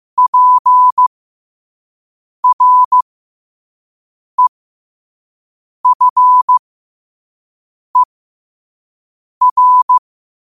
Morse code, heard as a keyed tone, about 15 wpm, spaced as for 5 wpm, 1000 Hz. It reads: PREFER
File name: qotdfiles / 5wpm